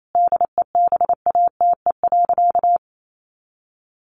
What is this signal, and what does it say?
Morse code, heard as a keyed tone, about 28 wpm, 700 Hz. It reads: DEBATE.
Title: Sentences from Top 1100 Words 28wpm